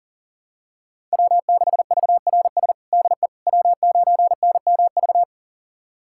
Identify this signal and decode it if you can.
Morse code, heard as a keyed tone, about 40 wpm, 700 Hz. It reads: W6VRS DE W9NMV
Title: USA Callsign DE Callsign - 5 Characters - 40wpm